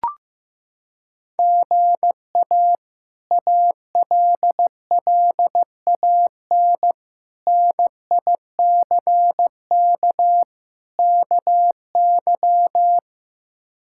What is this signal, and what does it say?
Morse code, heard as a keyed tone, about 15 wpm, 700 Hz. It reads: GA ALLAN NICK KY
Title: K1USN SST QSO Practice - 15wpm